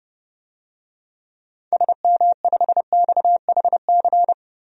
Morse code, heard as a keyed tone, about 32 wpm, 700 Hz. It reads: SM5XHC
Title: DXCC-100 Call Signs - 6 Characters - 30wpm